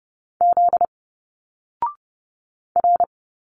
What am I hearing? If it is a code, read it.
Morse code, heard as a keyed tone, about 30 wpm, 700 Hz. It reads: Z R
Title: Single Letters 30wpm